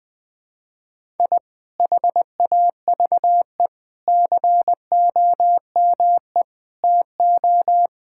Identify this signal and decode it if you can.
Morse code, heard as a keyed tone, about 20 wpm, 700 Hz. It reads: I HAVE COME TO
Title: Sentences with 7-Word Phrases - Spoken Beginning - 20wpm